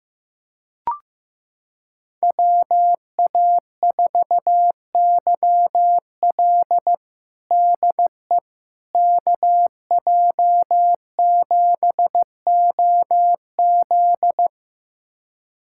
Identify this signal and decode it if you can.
Morse code, heard as a keyed tone, about 15 wpm, 700 Hz. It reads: WA4YL DE KJ7OZ
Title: USA Callsign DE Callsign 15wpm